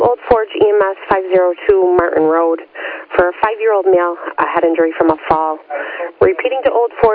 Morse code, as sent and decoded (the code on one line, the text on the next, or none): none